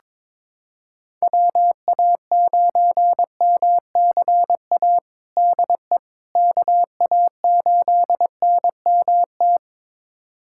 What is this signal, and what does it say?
Morse code, heard as a keyed tone, about 21 wpm, 700 Hz. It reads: WA9MCA DE KA8NMT
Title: USA Callsign DE Callsign - 6 Characters - 22wpm